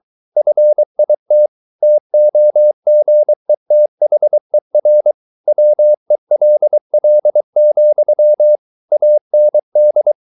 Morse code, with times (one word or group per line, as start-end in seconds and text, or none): 0.4-1.5 FIT
1.8-5.1 TOGETHER
5.5-8.6 WELL,
8.9-10.1 AND